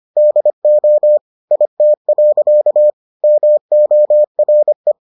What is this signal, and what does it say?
Morse code, heard as a keyed tone, about 25 wpm, 600 Hz. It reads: DO IT. MORE